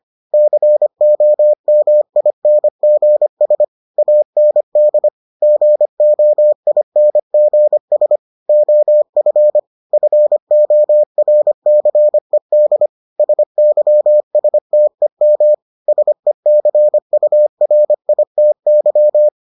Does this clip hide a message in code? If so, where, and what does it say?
Yes: Morse code from 0.3 to 19.3 s, heard as a keyed tone, about 25 wpm, 600 Hz. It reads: COMINGS AND GOINGS OF FORCED SYSTEM SECURITY